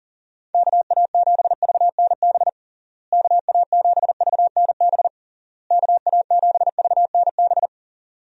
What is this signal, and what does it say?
Morse code, heard as a keyed tone, about 40 wpm, 700 Hz. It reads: KA7VNB KA7VNB KA7VNB
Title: USA Call Signs - 6 Characters - Repeated 3x - 40wpm